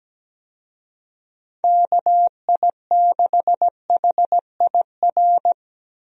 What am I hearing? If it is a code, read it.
Morse code, heard as a keyed tone, about 17 wpm, 700 Hz. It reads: KI6HIR
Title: USA Call Signs - 6 Characters - Repeated 3x - 17wpm